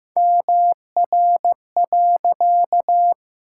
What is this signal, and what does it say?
Morse code, heard as a keyed tone, about 15 wpm, 700 Hz. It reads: MR.
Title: Pangram Challenge v2 - 15wpm